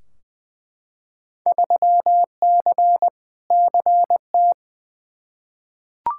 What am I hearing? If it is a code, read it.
Morse code, heard as a keyed tone, about 20 wpm, 700 Hz. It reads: 3C CT
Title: Field Day Exchanges Encore - 20wpm